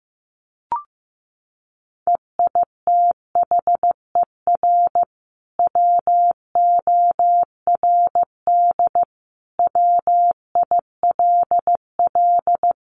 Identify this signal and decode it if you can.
Morse code, heard as a keyed tone, about 14 wpm, 700 Hz. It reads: EITHER WORD WILL
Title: Sentences from Top 600 Words 15wpm